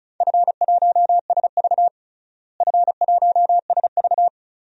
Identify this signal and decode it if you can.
Morse code, heard as a keyed tone, about 34 wpm, 700 Hz. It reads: F1SV F1SV
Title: DXCC-100 Call Signs - 4 Characters - Repeated 3x - 35wpm